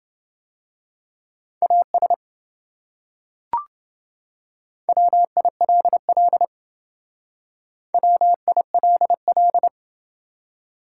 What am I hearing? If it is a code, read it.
Morse code, heard as a keyed tone, about 28 wpm, 700 Hz. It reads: AS WILL WILL